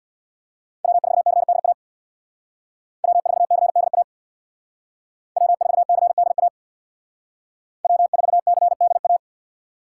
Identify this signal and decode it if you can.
Morse code, heard as a keyed tone, about 63 wpm, 700 Hz. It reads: W4CDA W4CDA W4CDA W4CDA